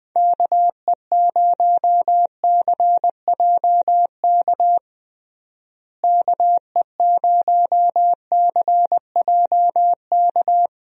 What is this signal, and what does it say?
Morse code, heard as a keyed tone, about 20 wpm, 700 Hz. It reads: KE0CJK KE0CJK
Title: USA Call Signs - 6 Characters - Repeated 2x - 20wpm